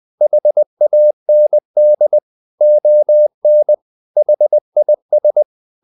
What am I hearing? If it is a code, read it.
Morse code, heard as a keyed tone, about 20 wpm, 600 Hz. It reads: HAND ON HIS
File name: the old man and the sea / the old man and the sea part 1 20wpm